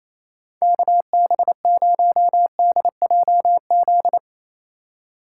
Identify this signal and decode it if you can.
Morse code, heard as a keyed tone, about 27 wpm, 700 Hz. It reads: KB0DJZ
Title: USA Call Signs - 6 Characters - 4x Character Spacing - 28wpm